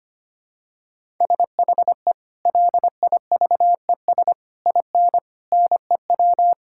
Morse code, heard as a keyed tone, about 25 wpm, 700 Hz. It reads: SHE LIVES IN NEW
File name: practice / 25wpm / Sentences with 3-Word Phrases - 25wpm